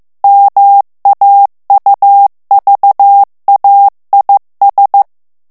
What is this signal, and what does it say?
Morse code, heard as a keyed tone, about 15 wpm, 800 Hz. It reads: MAUVAIS
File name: morse mp3 / FRANCAIS TEXTE CLAIR 1 VITESSE 14 MOTS MINUTES